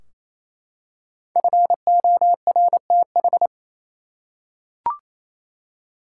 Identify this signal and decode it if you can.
Morse code, heard as a keyed tone, about 28 wpm, 700 Hz. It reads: FORTH
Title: Five Letter Words - 28wpm